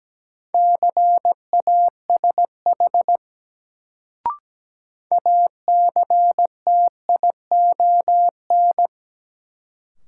Morse code, heard as a keyed tone, about 17 wpm, 700 Hz. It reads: CASH ACTION
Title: Sentences from Top 700 Words - New Word Drill - 17wpm